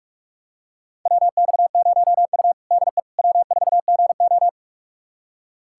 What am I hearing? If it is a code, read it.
Morse code, heard as a keyed tone, about 43 wpm, 700 Hz. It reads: WX0U DE W4GO
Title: USA Callsign DE Callsign - 4 Characters - 45wpm